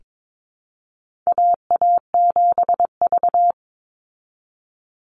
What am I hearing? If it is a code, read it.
Morse code, heard as a keyed tone, about 22 wpm, 700 Hz. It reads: AA7V
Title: Call Signs Repeated 2x - 22wpm